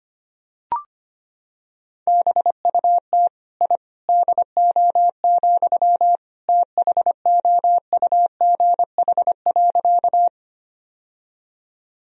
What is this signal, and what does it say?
Morse code, heard as a keyed tone, about 25 wpm, 700 Hz. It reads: BUT I DO, THOUGH.
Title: Sentences from Top 1000 Words - Review Part 2 - 25wpm